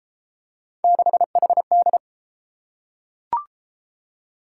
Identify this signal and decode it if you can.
Morse code, heard as a keyed tone, about 31 wpm, 700 Hz. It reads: 6HD